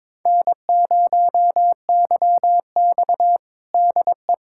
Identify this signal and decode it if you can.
Morse code, heard as a keyed tone, about 21 wpm, 700 Hz. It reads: N0YX DE
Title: USA Callsign DE Callsign - 4 Characters - 22wpm